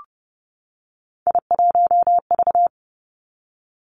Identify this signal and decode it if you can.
Morse code, heard as a keyed tone, about 30 wpm, 700 Hz. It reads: I1V
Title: Three Challenging Letter-Number Combinations 30wpm